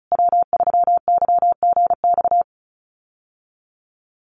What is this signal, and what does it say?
Morse code, heard as a keyed tone, about 34 wpm, 700 Hz. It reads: W3YGX
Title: USA Call Signs Repeated 3x 35wpm